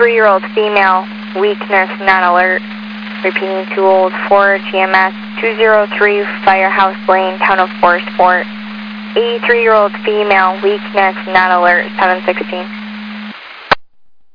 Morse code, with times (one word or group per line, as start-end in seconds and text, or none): none